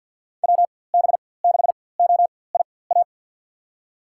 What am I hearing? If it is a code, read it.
Morse code, heard as a keyed tone, about 52 wpm, spaced as for 23 wpm, 700 Hz. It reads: WB6OIA
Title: USA Call Signs - 6 Characters - 4x Character Spacing - 50wpm